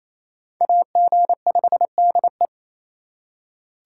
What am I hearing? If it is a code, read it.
Morse code, heard as a keyed tone, about 34 wpm, 700 Hz. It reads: AG5DE
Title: Speed Racing - USA 2x2 Call Signs 28wpm